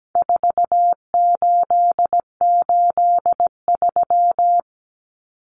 Morse code, heard as a keyed tone, about 17 wpm, 700 Hz. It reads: 4883
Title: Numbers - 4-Digit 17wpm